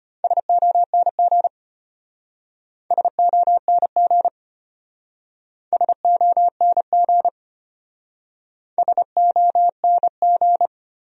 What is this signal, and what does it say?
Morse code, heard as a keyed tone, about 33 wpm, 700 Hz. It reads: SONG SONG SONG SONG